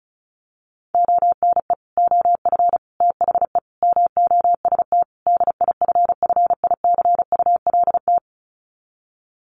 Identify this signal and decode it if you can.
Morse code, heard as a keyed tone, about 34 wpm, 700 Hz. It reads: ONE OF THE MOST DIFFICULT
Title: Sets of 5 Words Encore2 35wpm